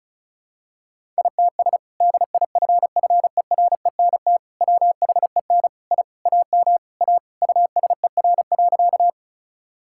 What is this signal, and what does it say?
Morse code, heard as a keyed tone, about 36 wpm, 700 Hz. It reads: ITS DIFFERENT WHEN I AM A USER.
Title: Sentences from Top 1000 Words - Review Part 5 - 35wpm